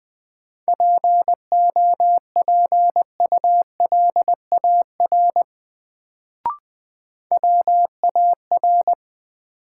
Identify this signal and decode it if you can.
Morse code, heard as a keyed tone, about 21 wpm, 700 Hz. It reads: POPULAR WAR
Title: Sets of 5 Words - New Word Drill - 20wpm